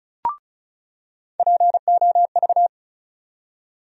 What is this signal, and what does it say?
Morse code, heard as a keyed tone, about 34 wpm, 700 Hz. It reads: POV
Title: USA Call Sign Postfixes - 3 Letters 35wpm